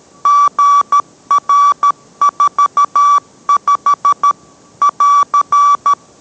Morse code, heard as a keyed tone, about 13 wpm, 1150 Hz. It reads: GR45 <AR>